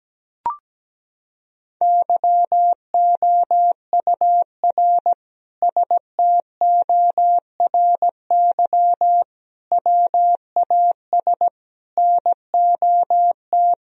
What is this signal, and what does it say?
Morse code, heard as a keyed tone, about 17 wpm, 700 Hz. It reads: YOUR STORY WAS NOT
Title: Sentences from Top 1000 Words - Review Part 2 - 17wpm